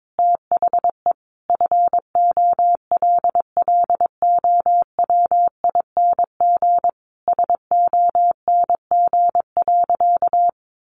Morse code, heard as a keyed tone, about 22 wpm, 700 Hz. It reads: THE FOLLOWING SONG.
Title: Sentences from Top 900 Words 22wpm